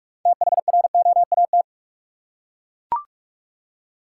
Morse code, heard as a keyed tone, about 41 wpm, 700 Hz. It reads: THROAT
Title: Six Letter Words - 45wpm